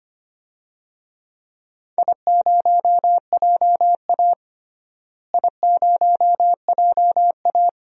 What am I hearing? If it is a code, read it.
Morse code, heard as a keyed tone, about 25 wpm, 700 Hz. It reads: I0JA I0JA
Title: DXCC-100 Call Signs - 4 Characters - Repeated 2x - 25wpm